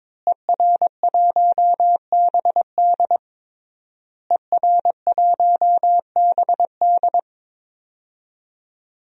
Morse code, heard as a keyed tone, about 22 wpm, 700 Hz. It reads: ER1BD ER1BD
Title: DXCC-100 Call Signs - 5 Characters - Repeated 3x - 22wpm